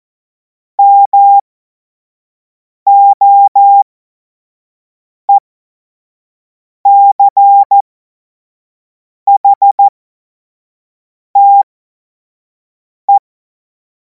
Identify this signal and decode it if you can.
Morse code, heard as a keyed tone, about 14 wpm, spaced as for 5 wpm, 800 Hz. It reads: MOECHTE
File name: Morsekurs / Witze / Witz 24. März 2026 25 F70